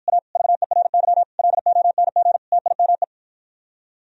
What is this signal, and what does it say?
Morse code, heard as a keyed tone, about 53 wpm, 700 Hz. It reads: A VERY LONG TIME